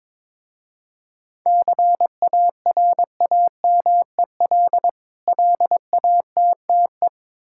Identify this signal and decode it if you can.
Morse code, heard as a keyed tone, about 21 wpm, 700 Hz. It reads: CARAMEL LATTE
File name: Difficult Words with Hints - 22wpm